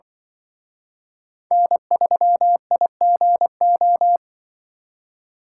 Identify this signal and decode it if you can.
Morse code, heard as a keyed tone, about 25 wpm, 700 Hz. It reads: N3IGO
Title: Speed Racing - USA 1x3 Call Signs 20wpm